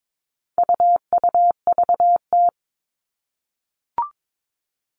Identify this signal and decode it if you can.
Morse code, heard as a keyed tone, about 22 wpm, 700 Hz. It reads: UUVT
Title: All Cut Numbers - 4-Digit 22wpm